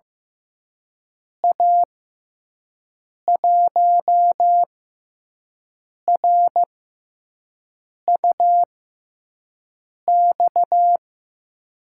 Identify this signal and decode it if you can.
Morse code, heard as a keyed tone, about 15 wpm, spaced as for 5 wpm, 700 Hz. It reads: A1RUX
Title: USA Call Signs - 6 Characters - 6x Character Spacing - 15wpm